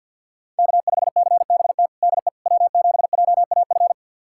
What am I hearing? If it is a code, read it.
Morse code, heard as a keyed tone, about 52 wpm, 700 Hz. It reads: K5CBT DE W7JAF
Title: USA Callsign DE Callsign - 5 Characters - 50wpm